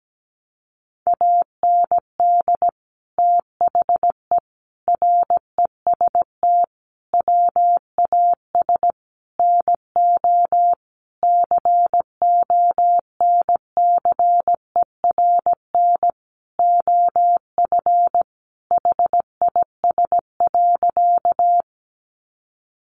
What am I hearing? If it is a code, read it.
Morse code, heard as a keyed tone, about 17 wpm, 700 Hz. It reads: AND THE REST WAS NO CONCERN OF HIS.
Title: Sentences from Top 1000 Words 17wpm